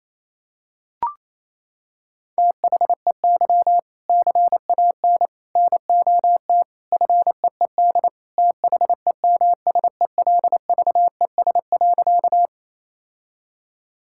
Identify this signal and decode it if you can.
Morse code, heard as a keyed tone, about 28 wpm, 700 Hz. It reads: THEY CAN NOT FEED THEMSELVES.
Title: Sentences from Top 1600 Words 28wpm